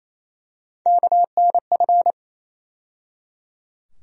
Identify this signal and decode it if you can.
Morse code, heard as a keyed tone, about 28 wpm, 700 Hz. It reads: KNF
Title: USA Call Sign Postfixes - 3 Letters 28wpm